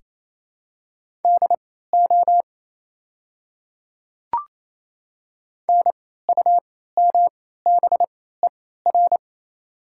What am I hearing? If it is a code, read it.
Morse code, heard as a keyed tone, about 28 wpm, spaced as for 16 wpm, 700 Hz. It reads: DO NUMBER
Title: 3x Character Spacing - Top 100 Words - 28wpm